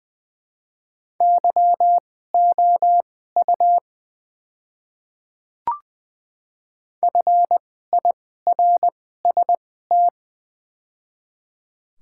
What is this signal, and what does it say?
Morse code, heard as a keyed tone, about 20 wpm, spaced as for 14 wpm, 700 Hz. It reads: YOU FIRST
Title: 2x Character Spacing - Top 100 Words - 20wpm